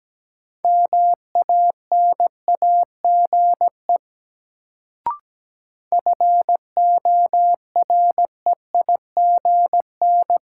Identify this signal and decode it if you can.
Morse code, heard as a keyed tone, about 17 wpm, 700 Hz. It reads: MANAGE FOREIGN
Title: Sentences from Top 1400 Words - New Word Drill Alt - 17wpm